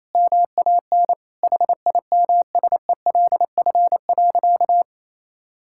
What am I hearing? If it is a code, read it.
Morse code, heard as a keyed tone, about 28 wpm, 700 Hz. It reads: MAN HIMSELF.
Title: Sentences from Top 1600 Words 28wpm